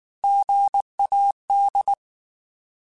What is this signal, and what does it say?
Morse code, heard as a keyed tone, about 19 wpm, 800 Hz. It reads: GAD